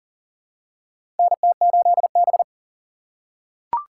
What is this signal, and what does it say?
Morse code, heard as a keyed tone, about 40 wpm, 690 Hz. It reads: NT8B
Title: USA Call Signs - 4 Characters - 2x Character Spacing - 40wpm